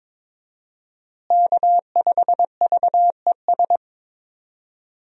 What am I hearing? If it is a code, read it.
Morse code, heard as a keyed tone, about 22 wpm, 700 Hz. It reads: K5VES
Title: USA Call Signs - 5 Characters - Repeated 3x - 22wpm